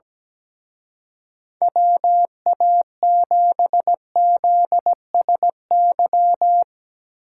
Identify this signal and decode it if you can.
Morse code, heard as a keyed tone, about 18 wpm, 700 Hz. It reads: WA7ZSY
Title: Speed Racing - USA 2x3 Call Signs 17wpm